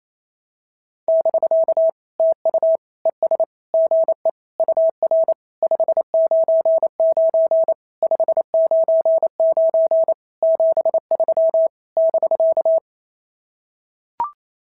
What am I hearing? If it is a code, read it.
Morse code, heard as a keyed tone, about 28 wpm, 650 Hz. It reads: <BK> TU ES GE UR 599 599 73 <BK>
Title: SOTA QSO Practice 28wpm